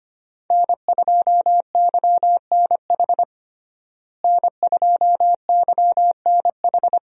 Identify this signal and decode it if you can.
Morse code, heard as a keyed tone, about 25 wpm, 700 Hz. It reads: N2YNH N2YNH
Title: USA Call Signs - 5 Characters - Repeated 3x - 25wpm